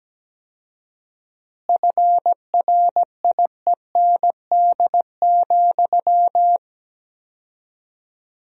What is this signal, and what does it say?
Morse code, heard as a keyed tone, about 17 wpm, 700 Hz. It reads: FRIEND,
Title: 4x Word-Spacing - Sentences from Top 500 Words - 17wpm